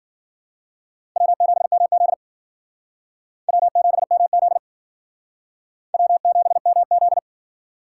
Morse code, heard as a keyed tone, about 52 wpm, 700 Hz. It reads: W7MZ W7MZ W7MZ